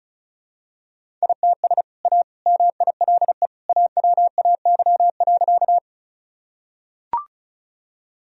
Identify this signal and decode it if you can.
Morse code, heard as a keyed tone, about 34 wpm, 700 Hz. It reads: ITS A MILE AWAY.
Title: Sentences from Top 1000 Words - Review Part 1 - 35wpm